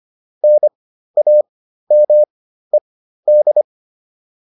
Morse code, heard as a keyed tone, about 25 wpm, spaced as for 13 wpm, 600 Hz. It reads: NAMED